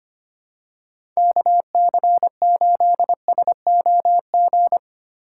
Morse code, heard as a keyed tone, about 25 wpm, 700 Hz. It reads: KC8SOG
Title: USA Call Signs - 2x Character Spacing - 25wpm